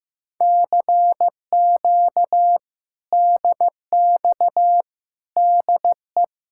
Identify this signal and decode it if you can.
Morse code, heard as a keyed tone, about 15 wpm, 700 Hz. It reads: CQ DX DE
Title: Speed-Racing Common QSO Phrases Encore - 15wpm